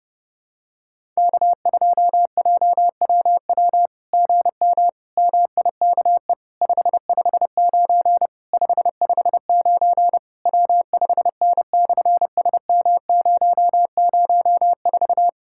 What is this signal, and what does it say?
Morse code, heard as a keyed tone, about 30 wpm, 700 Hz. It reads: K2JWW GM MIKE 559 559 W5N/SM004
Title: SOTA QSO Practice 30wpm